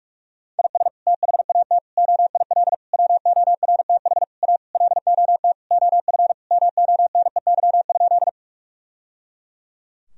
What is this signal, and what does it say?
Morse code, heard as a keyed tone, about 46 wpm, 700 Hz. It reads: IS THAT OIL WORTH A LOT OF MONEY?
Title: Sentences from Top 1000 Words - Review Part 4 - 45wpm